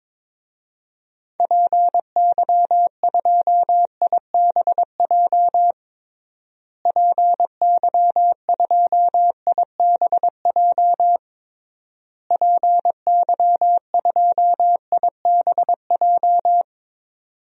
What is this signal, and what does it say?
Morse code, heard as a keyed tone, about 22 wpm, 700 Hz. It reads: PY2IBJ PY2IBJ PY2IBJ
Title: DXCC-100 Call Signs - 6 Characters - Repeated 3x - 22wpm